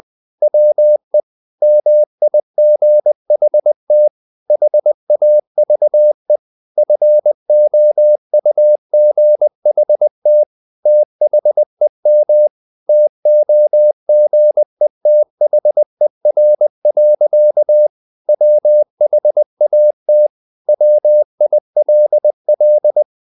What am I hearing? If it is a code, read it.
Morse code, heard as a keyed tone, about 20 wpm, 600 Hz. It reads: WE MIGHT HAVE FOUGHT THEM TOGETHER. WHAT WILL
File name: the old man and the sea part 2 20wpm